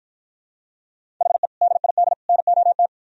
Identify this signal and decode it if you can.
Morse code, heard as a keyed tone, about 52 wpm, 700 Hz. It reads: HE DID NOT